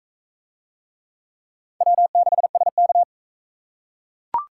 Morse code, heard as a keyed tone, about 39 wpm, 700 Hz. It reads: W6SK